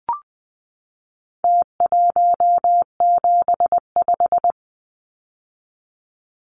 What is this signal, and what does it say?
Morse code, heard as a keyed tone, about 20 wpm, 700 Hz. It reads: T175
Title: Numbers - Contest Style - 4-Digit 20wpm